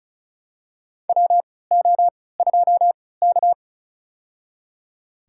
Morse code, heard as a keyed tone, about 34 wpm, spaced as for 20 wpm, 700 Hz. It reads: WO2K
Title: USA Call Signs - 4 Characters - 3x Character Spacing - 35wpm